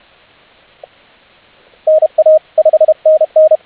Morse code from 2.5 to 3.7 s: 5NN